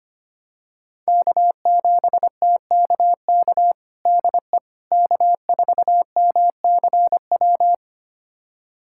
Morse code, heard as a keyed tone, about 25 wpm, 700 Hz. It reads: K7TKK DE K4MCW
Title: USA Callsign DE Callsign - 5 Characters - 25wpm